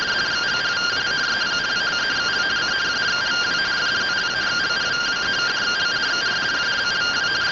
Two-tone teleprinter signal, about 170 Hz shift.
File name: findings / MR CORONAL fr1 -131.gen-18 rtty rx